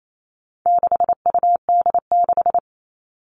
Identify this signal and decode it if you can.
Morse code, heard as a keyed tone, about 28 wpm, 700 Hz. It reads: 6UD6
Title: All Cut Numbers - 4-Digit 28wpm